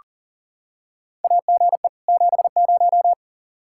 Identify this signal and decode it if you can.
Morse code, heard as a keyed tone, about 40 wpm, 700 Hz. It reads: AGE 70
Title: Practice QSO - Exchange 3 - More Info - 40wpm